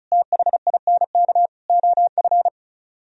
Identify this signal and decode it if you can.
Morse code, heard as a keyed tone, about 46 wpm, 700 Hz. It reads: THINK OF